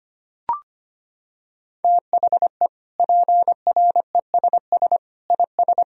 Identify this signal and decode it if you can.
Morse code, heard as a keyed tone, about 25 wpm, 700 Hz. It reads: THE PRESS IS
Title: Sentences from Top 1000 Words - Review Part 1 - 25wpm